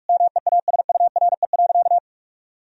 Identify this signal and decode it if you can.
Morse code, heard as a keyed tone, about 41 wpm, 700 Hz. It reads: MEASURE.
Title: Sentences from Top 1000 Words - Review Part 3 - 45wpm